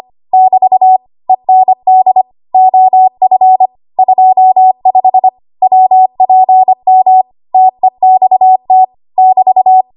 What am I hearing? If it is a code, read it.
Morse code, heard as a keyed tone, about 25 wpm, 750 Hz. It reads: <BT> END OF 25 WPM TEXT <BT>
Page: W1AW Web Code Practice Files